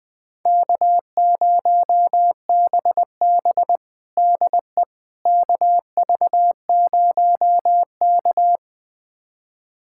Morse code, heard as a keyed tone, about 20 wpm, 700 Hz. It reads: K0BB DE KV0K
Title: USA Callsign DE Callsign - 4 Characters - 20wpm